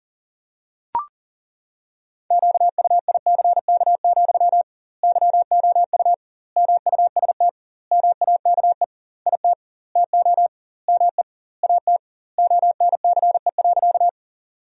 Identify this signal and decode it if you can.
Morse code, heard as a keyed tone, about 37 wpm, 700 Hz. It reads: QUICK, YOU MUST MAKE IT TO ME AT ONCE.
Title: 2x Word-Spacing - Sentences from Top 500 Words - 40wpm